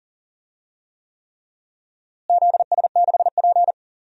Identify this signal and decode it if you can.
Morse code, heard as a keyed tone, about 40 wpm, 700 Hz. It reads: ZS6P
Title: DXCC-100 Call Signs - 4 Characters - Repeated 3x - 40wpm